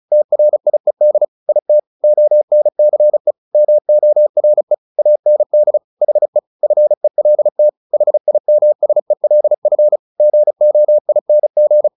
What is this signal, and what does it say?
Morse code, heard as a keyed tone, about 34 wpm, 600 Hz. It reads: TRIED IT ONCE MORE AND HE FELT HIMSELF GOING
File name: the old man and the sea 2 35wpm